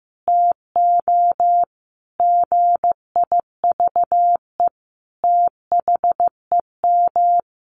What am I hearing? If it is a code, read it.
Morse code, heard as a keyed tone, about 15 wpm, 700 Hz. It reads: TO GIVE THEM
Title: Sentences from Top 1100 Words 15wpm